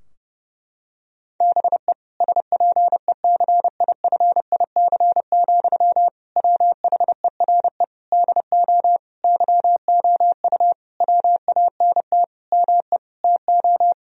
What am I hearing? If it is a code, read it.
Morse code, heard as a keyed tone, about 32 wpm, 700 Hz. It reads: BE SPECIFIC, WHERE DO YOU WANT ME TO
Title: Sentences from Top 1000 Words - Review Part 3 - 30wpm